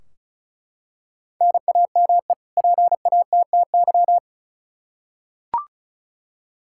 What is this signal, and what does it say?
Morse code, heard as a keyed tone, about 34 wpm, 700 Hz. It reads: NAME PATTY
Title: K1USN SST QSO Practice 35wpm